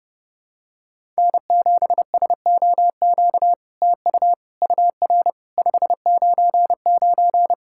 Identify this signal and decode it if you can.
Morse code, heard as a keyed tone, about 30 wpm, 700 Hz. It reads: N7SOQ TU UR 599
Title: SOTA QSO Practice 30wpm